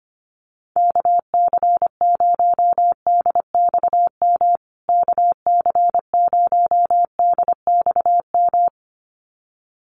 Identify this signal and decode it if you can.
Morse code, heard as a keyed tone, about 25 wpm, 700 Hz. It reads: KC0DXM KC0DXM
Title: USA Call Signs Repeated 2x 25wpm